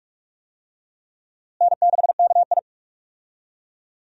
Morse code, heard as a keyed tone, about 46 wpm, 700 Hz. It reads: N6KI
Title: USA Call Signs - 4 Characters - Repeated 3x - 45wpm